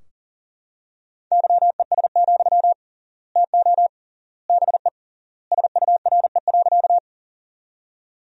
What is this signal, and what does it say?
Morse code, heard as a keyed tone, about 40 wpm, 700 Hz. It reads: YES, TO BE SURE.
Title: 3x Word-Spacing - Sentences from Top 500 Words - 40wpm